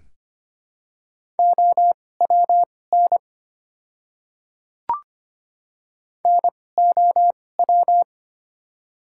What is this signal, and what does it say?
Morse code, heard as a keyed tone, about 25 wpm, spaced as for 18 wpm, 700 Hz. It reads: OWN NOW
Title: 2x Character Spacing - Top 300 Words - 25wpm